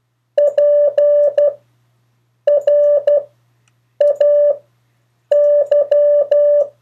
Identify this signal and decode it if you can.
Morse code, heard as a keyed tone, about 12 wpm, spaced as for 7 wpm, 580 Hz. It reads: PRAY
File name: 3of4-words-2